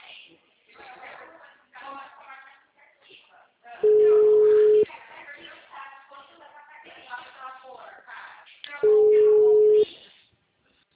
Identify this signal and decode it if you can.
Ringback tone (Europe, 425 Hz), with speech.
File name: out- -2028- - - .62047